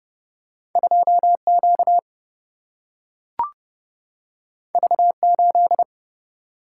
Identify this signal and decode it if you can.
Morse code, heard as a keyed tone, about 30 wpm, 700 Hz. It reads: 2Q V8
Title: Single-Letter Single-Number 30wpm